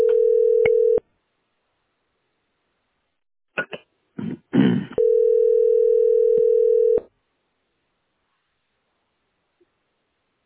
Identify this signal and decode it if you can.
Ringback tone (North America), 440 + 480 Hz.